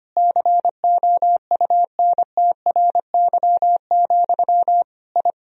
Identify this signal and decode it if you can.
Morse code, heard as a keyed tone, about 25 wpm, 700 Hz. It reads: COUNTRY, I
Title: Sentences from Top 1000 Words - Review Part 2 - 25wpm